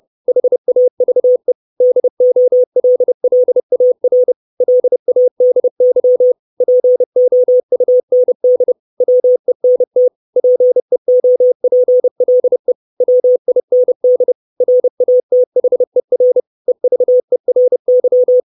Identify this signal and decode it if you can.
Morse code, heard as a keyed tone, about 30 wpm, 500 Hz. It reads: HAVE DOLLAR LADY POUND WENT PEOPLE WIND RATHER EVERY
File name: 632 Words 30WPM